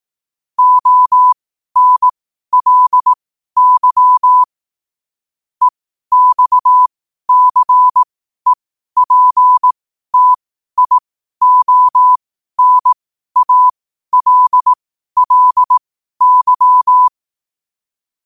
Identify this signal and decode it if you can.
Morse code, heard as a keyed tone, about 18 wpm, spaced as for 12 wpm, 1000 Hz. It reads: ONLY EXCEPTIONALLY
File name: qotdfiles / 12wpm